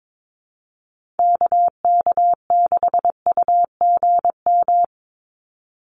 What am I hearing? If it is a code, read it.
Morse code, heard as a keyed tone, about 22 wpm, 700 Hz. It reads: KK6UGM
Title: USA Call Signs 22wpm